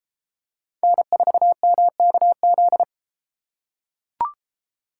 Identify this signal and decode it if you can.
Morse code, heard as a keyed tone, about 31 wpm, 700 Hz. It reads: N4MKZ